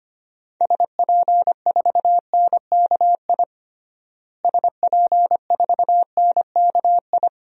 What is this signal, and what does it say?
Morse code, heard as a keyed tone, about 25 wpm, 700 Hz. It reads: SP4NKI SP4NKI
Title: DXCC-100 Call Signs - 6 Characters - Repeated 2x - 25wpm